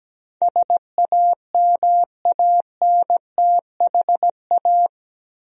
Morse code, heard as a keyed tone, about 17 wpm, 700 Hz. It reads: SAMANTHA
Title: USA Common Names - 17wpm